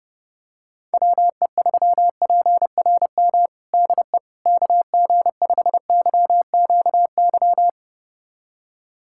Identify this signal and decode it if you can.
Morse code, heard as a keyed tone, about 30 wpm, 700 Hz. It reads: WE3PRM DE KG5YQY
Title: USA Callsign DE Callsign - 6 Characters - 30wpm